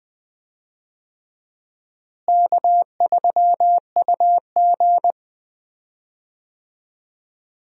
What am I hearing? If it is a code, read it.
Morse code, heard as a keyed tone, about 20 wpm, 700 Hz. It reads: K3UG
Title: USA Call Signs - 4 Characters - Repeated 3x - 20wpm